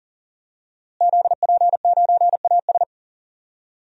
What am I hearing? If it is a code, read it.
Morse code, heard as a keyed tone, about 40 wpm, 700 Hz. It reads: ZP9AS
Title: DXCC-100 Call Signs - 5 Characters - Repeated 3x - 40wpm